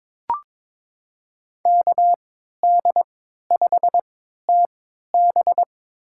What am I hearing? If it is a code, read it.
Morse code, heard as a keyed tone, about 22 wpm, spaced as for 12 wpm, 700 Hz. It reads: KD5TB
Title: USA Call Signs - 5 Characters - 3x Character Spacing - 22wpm